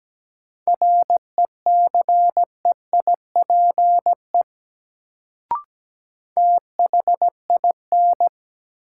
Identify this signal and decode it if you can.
Morse code, heard as a keyed tone, about 17 wpm, 700 Hz. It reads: RECEIPE THIN
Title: Sentences from Top 1200 Words - New Word Drill - 17wpm